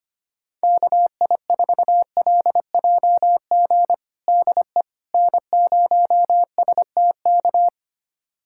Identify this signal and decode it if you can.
Morse code, heard as a keyed tone, about 25 wpm, 700 Hz. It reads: KI4LJG DE N0STK
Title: USA Callsign DE Callsign 25wpm